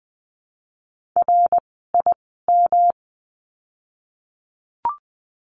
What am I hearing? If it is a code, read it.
Morse code, heard as a keyed tone, about 20 wpm, spaced as for 14 wpm, 700 Hz. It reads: RIM
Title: 108 Lesson 18f - Letter m - Words 20wpm